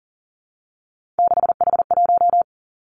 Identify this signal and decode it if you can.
Morse code, heard as a keyed tone, about 38 wpm, 700 Hz. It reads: 6H1
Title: Three Challenging Letter-Number Combinations 40wpm